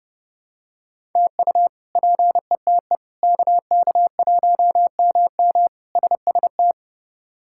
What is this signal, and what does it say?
Morse code, heard as a keyed tone, about 30 wpm, 700 Hz. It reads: TU PETE KK1MM SST
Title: K1USN SST QSO Practice - 30wpm